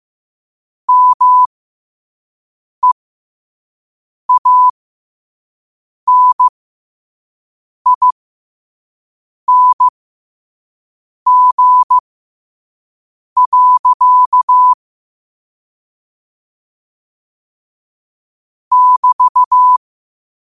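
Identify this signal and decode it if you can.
Morse code, heard as a keyed tone, about 15 wpm, spaced as for 5 wpm, 1000 Hz. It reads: MEANING. <BT>
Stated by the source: Morse code practice at 5 wpm.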